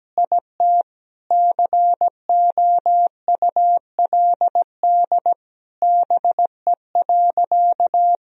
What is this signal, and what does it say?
Morse code, heard as a keyed tone, about 17 wpm, 700 Hz. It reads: IT COULD BE.
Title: Sentences from Top 1000 Words - Review Part 3 - 17wpm